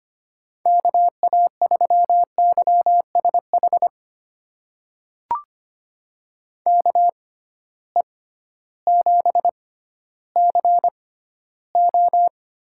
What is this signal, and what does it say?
Morse code, heard as a keyed tone, about 25 wpm, spaced as for 13 wpm, 700 Hz. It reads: KA3YSH KE7CO
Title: USA Call Signs - 6 Characters - 6x Character Spacing - 25wpm